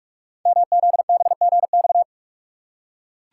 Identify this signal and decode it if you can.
Morse code, heard as a keyed tone, about 44 wpm, 690 Hz. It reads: MZBGX
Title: Code Groups - 5 Letters - 45wpm